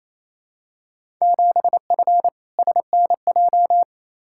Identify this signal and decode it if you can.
Morse code, heard as a keyed tone, about 28 wpm, 700 Hz. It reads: 7F SNJ
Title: Field Day Exchanges Encore - 28wpm